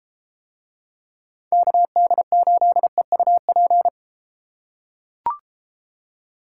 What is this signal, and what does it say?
Morse code, heard as a keyed tone, about 32 wpm, 700 Hz. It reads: KD8EUP